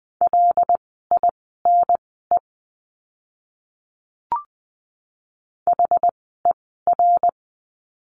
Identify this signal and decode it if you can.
Morse code, heard as a keyed tone, about 20 wpm, spaced as for 14 wpm, 700 Hz. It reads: LINE HER
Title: 115 Lesson 19f - Letter w - Words 20wpm